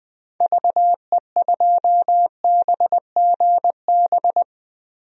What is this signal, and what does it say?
Morse code, heard as a keyed tone, about 20 wpm, 700 Hz. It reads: VE2BGB
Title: DXCC-100 Call Signs - 6 Characters - 20wpm